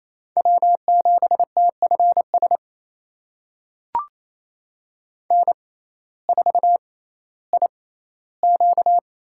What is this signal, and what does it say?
Morse code, heard as a keyed tone, about 27 wpm, spaced as for 15 wpm, 700 Hz. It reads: W7TFS N4IQ
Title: USA Call Signs - 6x Character Spacing - 28wpm